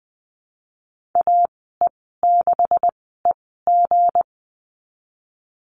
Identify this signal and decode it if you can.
Morse code, heard as a keyed tone, about 20 wpm, spaced as for 14 wpm, 700 Hz. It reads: AE6EG
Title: 238 Lesson 38g - Letter q - Call Signs 20wpm